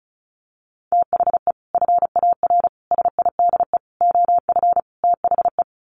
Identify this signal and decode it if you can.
Morse code, heard as a keyed tone, about 34 wpm, 700 Hz. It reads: THE FAR SIDE OF THE
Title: Sets of 5 Words 35wpm